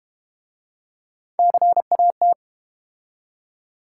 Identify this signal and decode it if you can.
Morse code, heard as a keyed tone, about 30 wpm, 700 Hz. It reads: CAT
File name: Easy Words - Speed-Racing with Hints - 25wpm